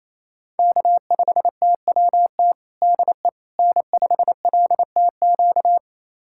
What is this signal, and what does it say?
Morse code, heard as a keyed tone, about 28 wpm, 700 Hz. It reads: K5TWT DE N5LTQ
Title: USA Callsign DE Callsign - 5 Characters - 28wpm